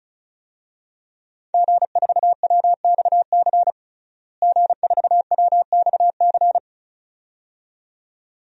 Morse code, heard as a keyed tone, about 36 wpm, 700 Hz. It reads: G4WXC G4WXC
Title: DXCC-100 Call Signs - 5 Characters - Repeated 2x - 35wpm